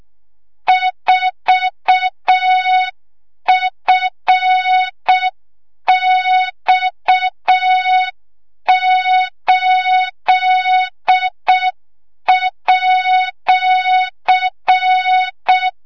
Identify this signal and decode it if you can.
Morse code, heard as a keyed tone, about 6 wpm, 3700 Hz, with noise. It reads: 4FX8@
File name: Morze-HA2MN / hang / 25 GYAKORLAT